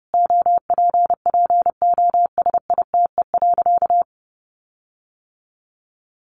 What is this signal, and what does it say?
Morse code, heard as a keyed tone, about 30 wpm, 700 Hz. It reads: OPPOSITE.
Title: Sentences from Top 1300 Words 30wpm